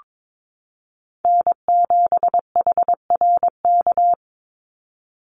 Morse code, heard as a keyed tone, about 22 wpm, 700 Hz. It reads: N7HRK
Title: Call Signs - 22wpm